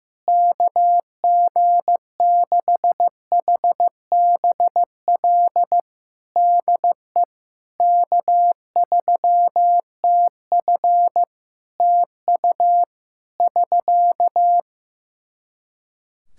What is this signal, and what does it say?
Morse code, heard as a keyed tone, about 15 wpm, 700 Hz. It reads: KG6HBL DE K3TF TU <SK>
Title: Practice QSO - Exchange 4 - Ending - 15wpm